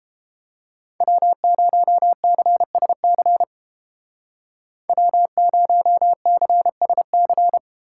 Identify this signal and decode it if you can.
Morse code, heard as a keyed tone, about 30 wpm, 690 Hz. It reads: W0CSC W0CSC